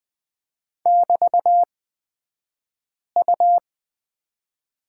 Morse code, heard as a keyed tone, about 20 wpm, spaced as for 5 wpm, 700 Hz. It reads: <BT>U
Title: 223 Lesson 36d - BT - Two Characters 20wpm